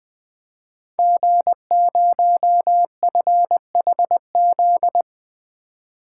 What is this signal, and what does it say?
Morse code, heard as a keyed tone, about 20 wpm, 700 Hz. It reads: G0FHZ
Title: Code Groups - 5 Letter:Numbers - 20wpm